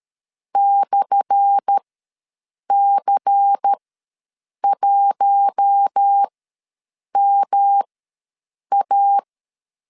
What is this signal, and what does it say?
Morse code, heard as a keyed tone, about 13 wpm, spaced as for 7 wpm, 800 Hz. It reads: /C1MA
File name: LEZIONI / 10 SIGNS / 10A SLASH ws 12wpm 3443